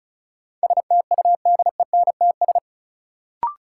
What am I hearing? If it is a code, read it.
Morse code, heard as a keyed tone, about 34 wpm, 700 Hz. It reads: STUDENTS
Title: Sets of 5 Words Encore - New Word Drill - 35wpm